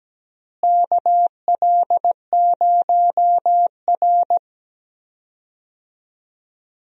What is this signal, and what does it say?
Morse code, heard as a keyed tone, about 17 wpm, 700 Hz. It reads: KL0R
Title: USA Call Signs - 4 Characters - Repeated 2x - 17wpm